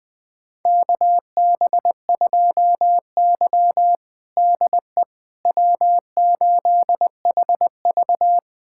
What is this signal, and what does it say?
Morse code, heard as a keyed tone, about 20 wpm, 700 Hz. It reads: KB2Y DE W8HV
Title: USA Callsign DE Callsign - 4 Characters - 20wpm